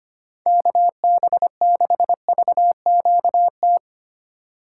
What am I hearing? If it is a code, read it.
Morse code, heard as a keyed tone, about 25 wpm, 700 Hz. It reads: KB6VQT
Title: USA Call Signs - 6 Characters - 4x Character Spacing - 25wpm